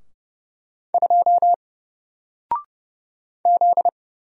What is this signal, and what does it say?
Morse code, heard as a keyed tone, about 30 wpm, 700 Hz. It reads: 2 Z
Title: Single Letter-Number Alt 30wpm